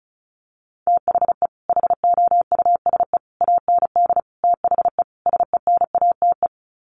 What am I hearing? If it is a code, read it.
Morse code, heard as a keyed tone, about 34 wpm, 700 Hz. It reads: THE HOUSE AND THE SENATE
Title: Sets of 5 Words Encore2 35wpm